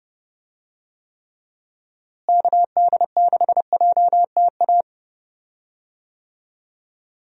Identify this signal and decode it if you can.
Morse code, heard as a keyed tone, about 31 wpm, 700 Hz. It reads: KD6JTA
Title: USA Call Signs - 6 Characters - Repeated 3x - 30wpm